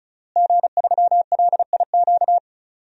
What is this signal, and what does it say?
Morse code, heard as a keyed tone, about 34 wpm, 700 Hz. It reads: G3LIQ
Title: DXCC-100 Call Signs - 5 Characters - Repeated 3x - 35wpm